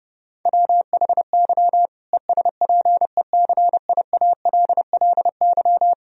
Morse code, heard as a keyed tone, about 30 wpm, 700 Hz. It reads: WHY ESPECIALLY
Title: Sentences from Top 1000 Words - Review Part 4 - 30wpm